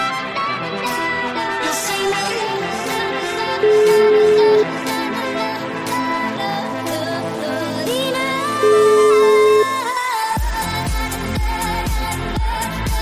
Ringback tone (Europe, 425 Hz), with noise.